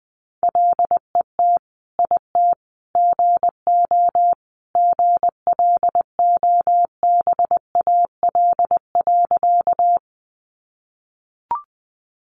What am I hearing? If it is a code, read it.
Morse code, heard as a keyed tone, about 20 wpm, 700 Hz. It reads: LET IT GO GLOBAL.
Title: Sentences from Top 1200 Words 20wpm